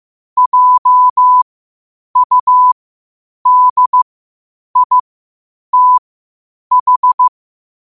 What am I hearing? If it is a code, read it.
Morse code, heard as a keyed tone, about 15 wpm, spaced as for 8 wpm, 1000 Hz. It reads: JUDITH
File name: qotdfiles / 8wpm